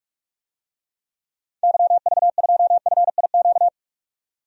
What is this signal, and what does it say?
Morse code, heard as a keyed tone, about 41 wpm, 700 Hz. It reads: YV2FIQ
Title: DXCC-100 Call Signs - 6 Characters - 45wpm